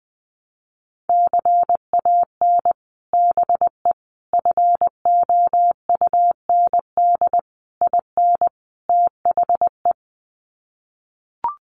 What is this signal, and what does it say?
Morse code, heard as a keyed tone, about 20 wpm, 700 Hz. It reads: CAN BE FOUND IN THE
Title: Sets of 5 Words Encore 20wpm